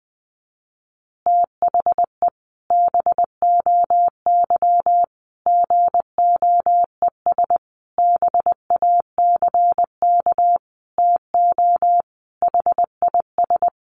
Morse code, heard as a keyed tone, about 20 wpm, 700 Hz. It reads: THE BOY GOES BACK TO HIS
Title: Additional Easy Phrases 20wpm